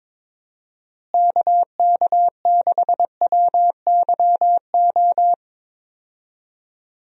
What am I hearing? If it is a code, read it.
Morse code, heard as a keyed tone, about 22 wpm, 700 Hz. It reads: KK6WYO
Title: USA Call Signs - 6 Characters - Repeated 2x - 22wpm